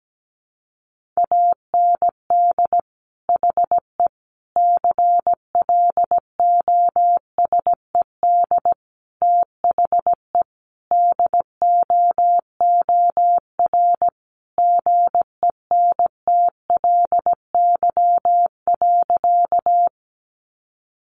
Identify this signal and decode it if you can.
Morse code, heard as a keyed tone, about 17 wpm, 700 Hz. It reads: AND HE CLOSED THE DOOR GENTLY.
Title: Sentences from Top 1500 Words 17wpm